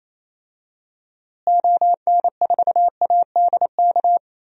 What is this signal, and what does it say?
Morse code, heard as a keyed tone, about 28 wpm, 700 Hz. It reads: ON4ADK
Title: DXCC-100 Call Signs - 6 Characters - Repeated 3x - 28wpm